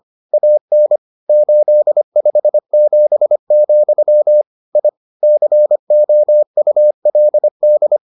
Morse code, heard as a keyed tone, about 25 wpm, 600 Hz. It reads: AN 857, I COULD